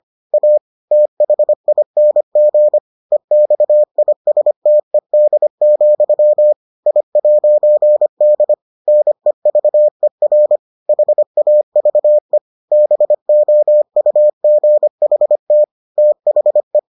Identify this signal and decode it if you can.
Morse code, heard as a keyed tone, about 25 wpm, 600 Hz. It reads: A THING EXISTED, I'D NEVER HAVE BOUGHT THE